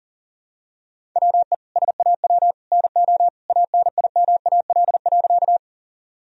Morse code, heard as a keyed tone, about 40 wpm, 700 Hz. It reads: WE SAW NO ANIMAL.
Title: Sentences from Top 1000 Words - Review Part 1 - 40wpm